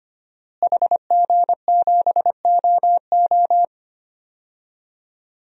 Morse code, heard as a keyed tone, about 25 wpm, 700 Hz. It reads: HG7OO
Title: DXCC-100 Call Signs - 5 Characters - Repeated 3x - 25wpm